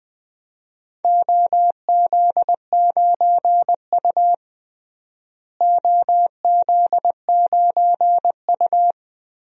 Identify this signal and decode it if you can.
Morse code, heard as a keyed tone, about 20 wpm, 700 Hz. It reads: OZ9U OZ9U
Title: DXCC-100 Call Signs - 4 Characters - Repeated 2x - 20wpm